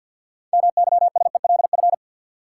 Morse code, heard as a keyed tone, about 47 wpm, 700 Hz. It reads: MYSELF
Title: Sentences from Top 1300 Words - New Word Drill - 50wpm